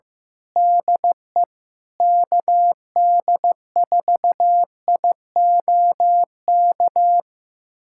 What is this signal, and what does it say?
Morse code, heard as a keyed tone, about 15 wpm, 700 Hz. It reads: DE KD4IOK
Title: USA Callsign DE Callsign 15wpm